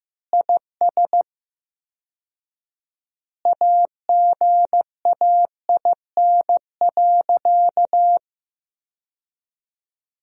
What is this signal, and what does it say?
Morse code, heard as a keyed tone, about 16 wpm, 700 Hz. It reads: IS AGAIN.
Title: 4x Word-Spacing - Sentences from Top 200 Words - 15wpm